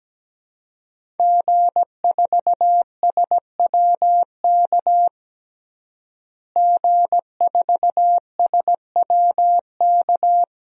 Morse code, heard as a keyed tone, about 17 wpm, 700 Hz. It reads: G4SWK G4SWK
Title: DXCC-100 Call Signs - 5 Characters - Repeated 3x - 17wpm